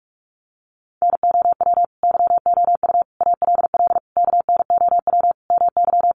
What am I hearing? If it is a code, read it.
Morse code, heard as a keyed tone, about 44 wpm, 700 Hz. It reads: NOW YOU ALL KNOW MY
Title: Sentences from Top 1200 Words 45wpm